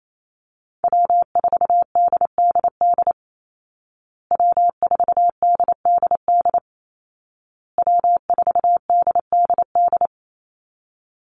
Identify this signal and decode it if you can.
Morse code, heard as a keyed tone, about 28 wpm, 700 Hz. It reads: W4DDD W4DDD W4DDD
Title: Call Signs Repeated 2x - 28wpm